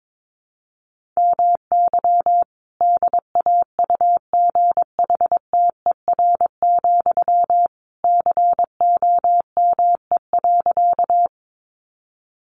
Morse code, heard as a keyed tone, about 22 wpm, 700 Hz. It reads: MY DAUGHTER, COME.
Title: Sentences from Top 1500 Words 22wpm